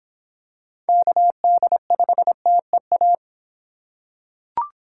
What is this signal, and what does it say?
Morse code, heard as a keyed tone, about 25 wpm, 700 Hz. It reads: KD5TEA